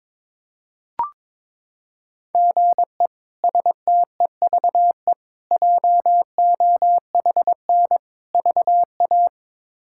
Morse code, heard as a keyed tone, about 22 wpm, 700 Hz. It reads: GE STEVE JOHN VA
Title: K1USN SST QSO Practice - 22wpm